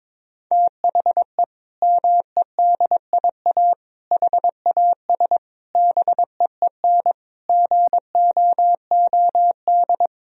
Morse code, heard as a keyed tone, about 22 wpm, 700 Hz. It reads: THE MEDIA HAS BEEN GOOD
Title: Sentences from Top 1000 Words - Review Part 4 - 22wpm